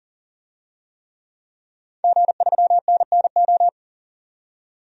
Morse code, heard as a keyed tone, about 40 wpm, 700 Hz. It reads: G3NNO
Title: DXCC-100 Call Signs - 5 Characters - 40wpm